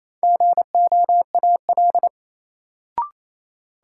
Morse code, heard as a keyed tone, about 28 wpm, 700 Hz. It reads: GOAL
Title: Sentences from Top 1000 Words - Review Part 5 - New Word Drill - 28wpm